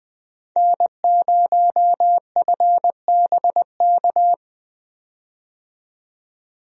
Morse code, heard as a keyed tone, about 20 wpm, 700 Hz. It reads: N0FBK
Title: USA Call Signs - 5 Characters - 20wpm